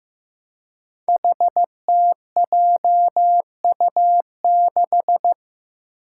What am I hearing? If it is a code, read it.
Morse code, heard as a keyed tone, about 15 wpm, 700 Hz. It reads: HTJU6
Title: Five Challenging Letter-Number Combinations 15wpm